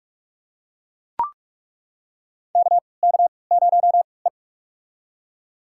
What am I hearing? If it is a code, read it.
Morse code, heard as a keyed tone, about 41 wpm, spaced as for 25 wpm, 700 Hz. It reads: KK0E
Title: USA Call Signs - 4 Characters - 3x Character Spacing - 45wpm